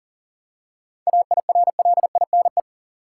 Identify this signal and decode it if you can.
Morse code, heard as a keyed tone, about 40 wpm, 700 Hz. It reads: AIRLINE
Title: Seven Letter Words - 40wpm